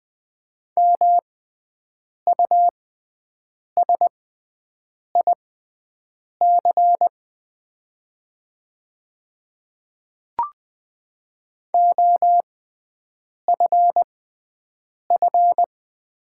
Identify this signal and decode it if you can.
Morse code, heard as a keyed tone, about 20 wpm, spaced as for 7 wpm, 690 Hz. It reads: MUSIC OFF
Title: 6x Character Spacing - Top 300 Words - 20wpm